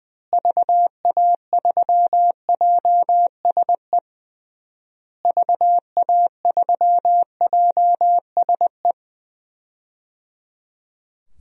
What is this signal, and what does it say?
Morse code, heard as a keyed tone, about 21 wpm, 700 Hz. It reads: VA3JSE VA3JSE
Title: DXCC-100 Call Signs - 6 Characters - Repeated 3x - 20wpm